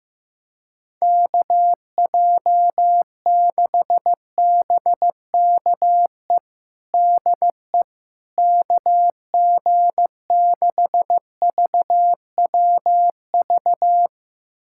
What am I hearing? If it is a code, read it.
Morse code, heard as a keyed tone, about 15 wpm, 700 Hz. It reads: KJ6BKE DE KG6VWV
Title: USA Callsign DE Callsign - 6 Characters - 15wpm